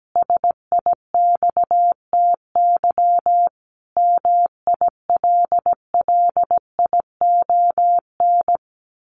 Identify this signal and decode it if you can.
Morse code, heard as a keyed tone, about 17 wpm, 700 Hz. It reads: SIXTY MILLION
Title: A Sound Of Thunder 17wpm